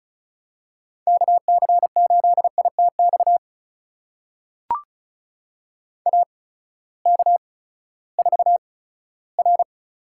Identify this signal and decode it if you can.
Morse code, heard as a keyed tone, about 36 wpm, spaced as for 18 wpm, 700 Hz. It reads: KC8ITX AK4R
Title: USA Call Signs - 8x Character Spacing - 35wpm